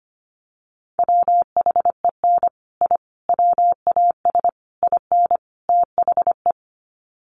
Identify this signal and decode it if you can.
Morse code, heard as a keyed tone, about 25 wpm, 700 Hz. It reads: WHEN I WAS IN THE
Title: Sets of 5 Words Encore2 25wpm